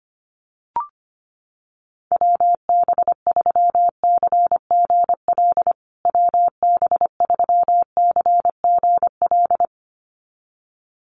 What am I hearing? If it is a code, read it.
Morse code, heard as a keyed tone, about 25 wpm, 700 Hz. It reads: WB3CGL WB3CGL
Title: USA Call Signs Repeated 2x 25wpm